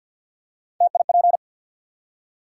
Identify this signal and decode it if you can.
Morse code, heard as a keyed tone, about 53 wpm, 700 Hz. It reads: TIP
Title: Three Letter Words - 50wpm